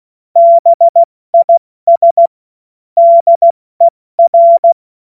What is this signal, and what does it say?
Morse code, heard as a keyed tone, about 16 wpm, spaced as for 14 wpm, 680 Hz. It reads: BIS DER
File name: Morsekurs / Witze / Witz 25. Mai 2021, 70 F80